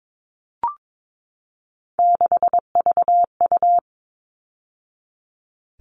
Morse code, heard as a keyed tone, about 22 wpm, 700 Hz. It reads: 6VU
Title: All Cut Numbers - 3-Digit 22wpm